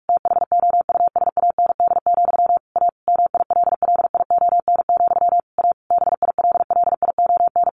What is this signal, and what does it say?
Morse code, heard as a keyed tone, about 44 wpm, 700 Hz. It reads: THOUSAND, A MILLION, A BILLION
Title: A Sound Of Thunder 45wpm